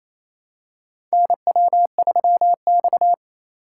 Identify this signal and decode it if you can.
Morse code, heard as a keyed tone, about 28 wpm, 700 Hz. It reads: NW3X
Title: USA Call Signs - 3x Character Spacing - 28wpm